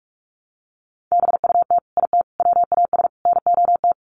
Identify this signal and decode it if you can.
Morse code, heard as a keyed tone, about 44 wpm, 700 Hz. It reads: BUT IT WAS NOT
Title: Sets of 4 Words 45wpm